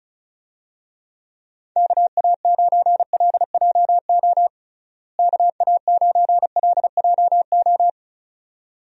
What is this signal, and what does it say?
Morse code, heard as a keyed tone, about 36 wpm, 700 Hz. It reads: KA9LJO KA9LJO
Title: USA Call Signs - 6 Characters - Repeated 2x - 35wpm